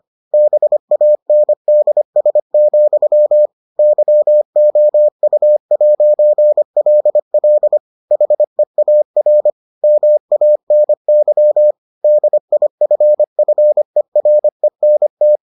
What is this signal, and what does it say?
Morse code, heard as a keyed tone, about 25 wpm, 600 Hz. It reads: BANDS, YOU'LL HEAR MANY DIFFERENT